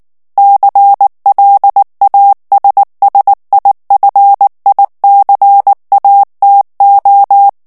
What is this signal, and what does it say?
Morse code, heard as a keyed tone, about 19 wpm, 800 Hz. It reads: CLASSIFICATO
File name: morse mp3 / ITALIEN texte clair 2 vitesse 18 mots minute